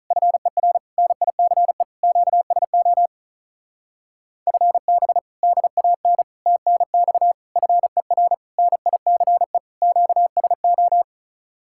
Morse code, heard as a keyed tone, about 47 wpm, 700 Hz. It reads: FER NICE QSO FB DAN TNX FER NICE QSO